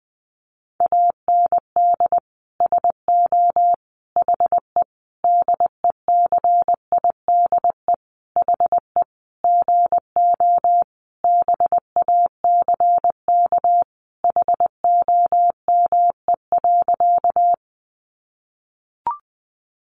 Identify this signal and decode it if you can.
Morse code, heard as a keyed tone, about 19 wpm, 700 Hz. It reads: AND SO HE DECIDE HE GO BACK HOME.
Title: Sentences from Top 500 Words 20wpm